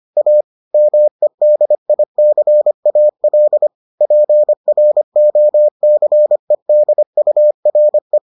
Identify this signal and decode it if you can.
Morse code, heard as a keyed tone, about 25 wpm, 600 Hz. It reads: A MEDICAL PROCEDURE